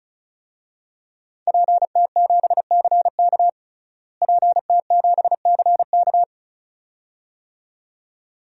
Morse code, heard as a keyed tone, about 36 wpm, 700 Hz. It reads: PT7CK PT7CK
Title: DXCC-100 Call Signs - 5 Characters - Repeated 2x - 35wpm